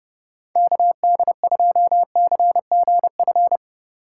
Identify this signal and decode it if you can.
Morse code, heard as a keyed tone, about 38 wpm, 700 Hz. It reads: KD2CGF